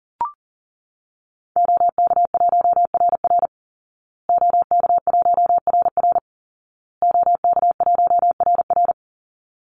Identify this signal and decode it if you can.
Morse code, heard as a keyed tone, about 40 wpm, 700 Hz. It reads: OK1RR OK1RR OK1RR
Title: Call Signs Repeated 2x - 40wpm